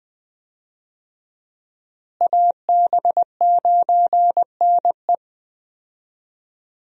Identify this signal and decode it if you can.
Morse code, heard as a keyed tone, about 21 wpm, 700 Hz. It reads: AB9NE
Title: USA Call Signs - 5 Characters - Repeated 3x - 20wpm